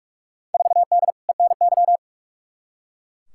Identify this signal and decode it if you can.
Morse code, heard as a keyed tone, about 41 wpm, 700 Hz. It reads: 4D ENY
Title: Field Day Exchanges Encore - 45wpm